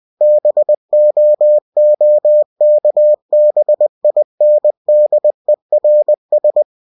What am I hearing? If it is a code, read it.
Morse code, heard as a keyed tone, about 20 wpm, 600 Hz. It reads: BOOKBINDERS
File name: cw-books / winnie-the-pooh / winnie-20wpm-page-0111